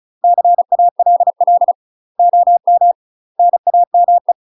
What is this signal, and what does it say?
Morse code, heard as a keyed tone, about 36 wpm, 700 Hz. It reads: CALL OM NAME
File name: UK MASTER QSOs TRAINING / 35WPM / UK MASTER QSOs 0016